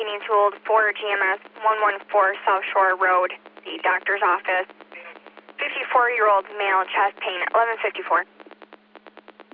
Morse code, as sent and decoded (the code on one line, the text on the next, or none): none